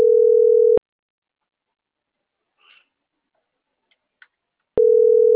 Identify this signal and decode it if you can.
Ringback tone (North America), 440 + 480 Hz.
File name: CENTRAL MAERSK / out luis.herrera